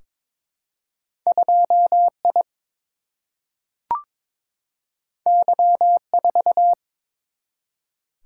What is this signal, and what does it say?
Morse code, heard as a keyed tone, about 22 wpm, 700 Hz. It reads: 2I Y4
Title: Single-Letter Single-Number 22wpm